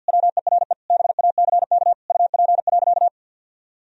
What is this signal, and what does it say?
Morse code, heard as a keyed tone, about 51 wpm, 700 Hz. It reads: WERE BACK UP.
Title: Sentences from Top 1000 Words - Review Part 2 - 50wpm